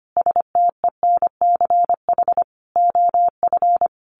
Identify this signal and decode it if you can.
Morse code, heard as a keyed tone, about 25 wpm, 700 Hz. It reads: STENCH OF
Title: A Sound Of Thunder 25wpm